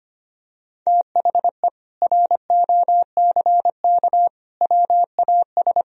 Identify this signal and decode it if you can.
Morse code, heard as a keyed tone, about 25 wpm, 700 Hz. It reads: THE ROCK WAS
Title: Sentences from Top 400 Words and Abbreviations 25wpm